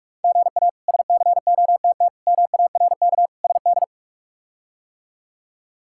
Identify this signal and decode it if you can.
Morse code, heard as a keyed tone, about 41 wpm, 700 Hz. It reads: GA SCOTT MARK SD
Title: K1USN SST QSO Practice - 45wpm